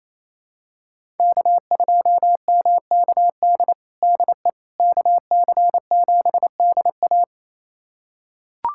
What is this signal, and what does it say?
Morse code, heard as a keyed tone, about 28 wpm, 700 Hz. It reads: K2MKD DE KC7DA
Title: USA Callsign DE Callsign - 5 Characters - 28wpm